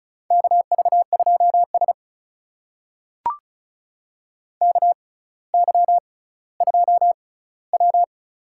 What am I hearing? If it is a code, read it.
Morse code, heard as a keyed tone, about 36 wpm, spaced as for 18 wpm, 700 Hz. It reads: KV2S KY2W
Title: USA Call Signs - 4 Characters - 6x Character Spacing - 35wpm